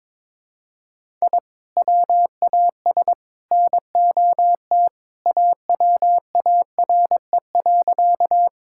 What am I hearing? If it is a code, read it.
Morse code, heard as a keyed tone, about 22 wpm, 700 Hz. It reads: I WAS NOT AWARE.
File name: practice / 22wpm / Sentences with 3-Word Phrases - 22wpm